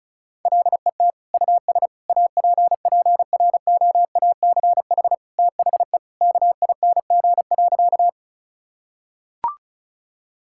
Morse code, heard as a keyed tone, about 34 wpm, 700 Hz. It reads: LET US APPROACH THE KING.
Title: Sentences from Top 1000 Words - Review Part 6 - 35wpm